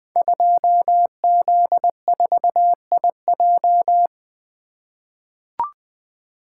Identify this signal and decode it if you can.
Morse code, heard as a keyed tone, about 20 wpm, 700 Hz. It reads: 2Z4IJ
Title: Code Groups - 5 Letter:Numbers - 20wpm